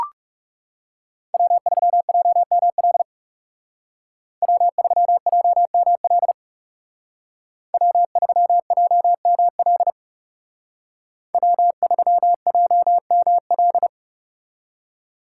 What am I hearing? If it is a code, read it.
Morse code, heard as a keyed tone, about 37 wpm, 700 Hz. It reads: W3JML W3JML W3JML W3JML
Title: Speed Racing - USA 1x3 Call Signs 30wpm